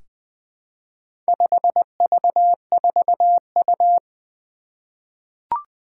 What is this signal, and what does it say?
Morse code, heard as a keyed tone, about 20 wpm, 700 Hz. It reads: 5V4U
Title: Four Challenging Letter-Number Combinations 20wpm